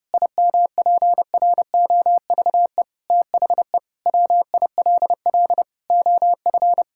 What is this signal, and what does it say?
Morse code, heard as a keyed tone, about 30 wpm, 700 Hz. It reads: IMPROVE THE WILL OF
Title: Sentences from Top 1000 Words - Review Part 5 - 30wpm